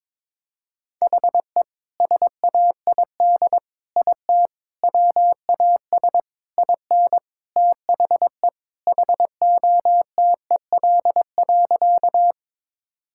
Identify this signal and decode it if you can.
Morse code, heard as a keyed tone, about 22 wpm, 700 Hz. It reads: HE SAID IT WAS IN THE HOTEL.
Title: Sentences from Top 1600 Words 22wpm